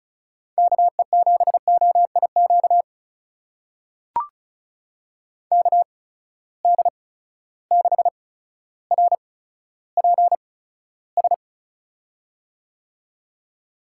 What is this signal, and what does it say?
Morse code, heard as a keyed tone, about 36 wpm, spaced as for 15 wpm, 700 Hz. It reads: KE7OIQ KD6RPS
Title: USA Call Signs - 6 Characters - 8x Character Spacing - 35wpm